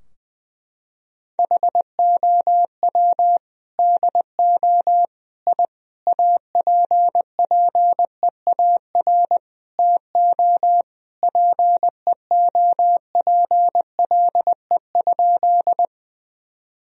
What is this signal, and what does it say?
Morse code, heard as a keyed tone, about 20 wpm, 700 Hz. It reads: HOW DO I APPEAR TO PEOPLE?
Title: Sentences from Top 1000 Words - Review Part 2 - 20wpm